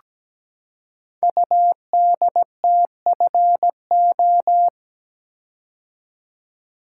Morse code, heard as a keyed tone, about 17 wpm, 700 Hz. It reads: UDTFO
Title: Code Groups - 5 Letters - 17wpm